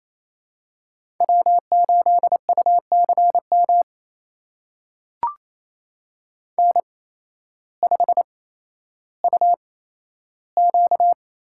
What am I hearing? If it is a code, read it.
Morse code, heard as a keyed tone, about 28 wpm, spaced as for 13 wpm, 700 Hz. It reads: W8UCM N5UQ
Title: USA Call Signs - 5 Characters - 8x Character Spacing - 28wpm